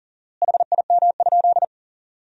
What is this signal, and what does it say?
Morse code, heard as a keyed tone, about 40 wpm, 700 Hz. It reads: HIM?
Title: 4x Word-Spacing - Sentences from Top 200 Words - 40wpm